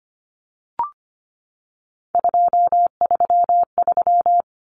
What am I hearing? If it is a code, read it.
Morse code, heard as a keyed tone, about 25 wpm, 700 Hz. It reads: 233
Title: Numbers - 3-Digit 25wpm